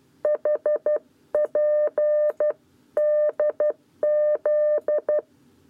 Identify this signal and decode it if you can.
Morse code, heard as a keyed tone, about 12 wpm, 600 Hz. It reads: HP DZ
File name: 7b lezione cw 1